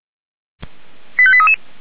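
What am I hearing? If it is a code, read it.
Five-tone selective call (ZVEI), heard as quick runs of five tones, tones of about 70 ms. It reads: 85833
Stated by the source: ZVEI five-tone call received on SDR, code 89343.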